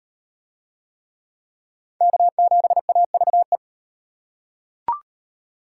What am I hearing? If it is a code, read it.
Morse code, heard as a keyed tone, about 33 wpm, 700 Hz. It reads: K7AVE